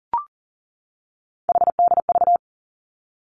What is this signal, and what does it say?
Morse code, heard as a keyed tone, about 40 wpm, 700 Hz. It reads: HDV
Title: Three Challenging Letter-Number Combinations 40wpm